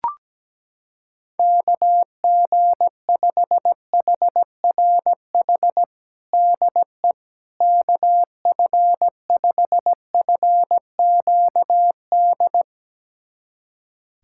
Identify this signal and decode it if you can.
Morse code, heard as a keyed tone, about 17 wpm, 700 Hz. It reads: KG5HRH DE KF5FQD
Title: USA Callsign DE Callsign - 6 Characters - 17wpm